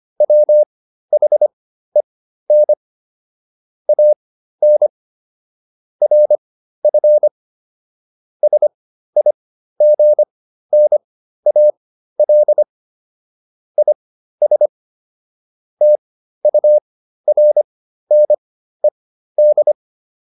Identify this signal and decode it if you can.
Morse code, heard as a keyed tone, about 25 wpm, spaced as for 13 wpm, 600 Hz. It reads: WHEN AN RF SIGNAL IS TURNED